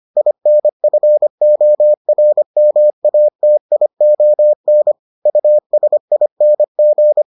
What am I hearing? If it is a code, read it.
Morse code, heard as a keyed tone, about 25 wpm, 600 Hz. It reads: INFORMATION USING